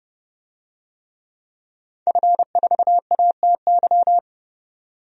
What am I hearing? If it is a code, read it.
Morse code, heard as a keyed tone, about 30 wpm, 700 Hz. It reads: F4ATY
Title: DXCC-100 Call Signs - 5 Characters - 30wpm